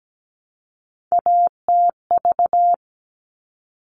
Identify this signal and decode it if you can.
Morse code, heard as a keyed tone, about 17 wpm, 700 Hz. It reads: ATV
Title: All Cut Numbers - 3-Digit 17wpm